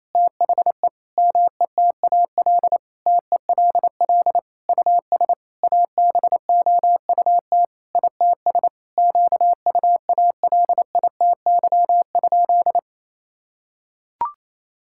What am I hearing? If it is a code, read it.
Morse code, heard as a keyed tone, about 28 wpm, 700 Hz. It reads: THE METAL TELL US ABOUT ITS QUALITY?
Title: Sentences from Top 1000 Words - Review Part 6 - 28wpm